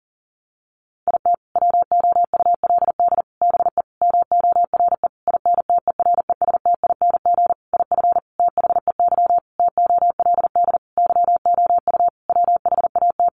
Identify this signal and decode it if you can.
Morse code, heard as a keyed tone, about 37 wpm, 700 Hz. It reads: IT WOULD BE MORE INTERESTING IF THEY TOLD YOU WHAT
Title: Sentences from Top 1000 Words 40wpm